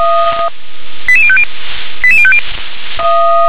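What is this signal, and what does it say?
Five-tone selective call (ZVEI), heard as quick runs of five tones, tones of about 70 ms. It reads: 88050 88050
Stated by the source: Five-tone ZVEI selective call 88163 received by SDR.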